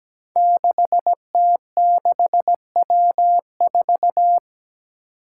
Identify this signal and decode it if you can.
Morse code, heard as a keyed tone, about 17 wpm, 700 Hz. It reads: 6T6W4
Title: Five Challenging Letter-Number Combinations 17wpm